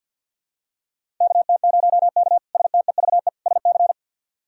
Morse code, heard as a keyed tone, about 52 wpm, 700 Hz. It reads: KT0K STEVE SC
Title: K1USN SST QSO Practice - 50wpm